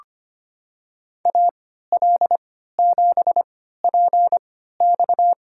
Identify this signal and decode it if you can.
Morse code, heard as a keyed tone, about 25 wpm, spaced as for 14 wpm, 700 Hz. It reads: AL7PX
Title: USA Call Signs - 3x Character Spacing - 25wpm